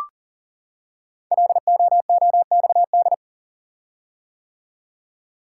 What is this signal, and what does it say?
Morse code, heard as a keyed tone, about 38 wpm, 690 Hz. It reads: LOOXD
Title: Code Groups - 5 Letter:Numbers - 40wpm